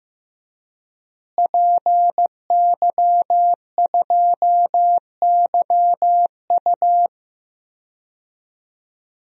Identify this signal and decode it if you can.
Morse code, heard as a keyed tone, about 15 wpm, 700 Hz. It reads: PY2YU
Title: DXCC-100 Call Signs - 5 Characters - Repeated 3x - 15wpm